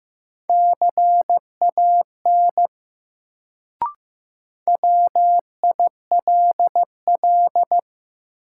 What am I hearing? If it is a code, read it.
Morse code, heard as a keyed tone, about 15 wpm, 700 Hz. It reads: CAN WILL
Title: Sentences from Top 100 Words - New Word Drill - 15wpm